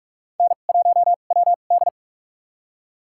Morse code, heard as a keyed tone, about 42 wpm, spaced as for 31 wpm, 700 Hz. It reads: N1WD
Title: USA Call Signs - 4 Characters - 2x Character Spacing - 45wpm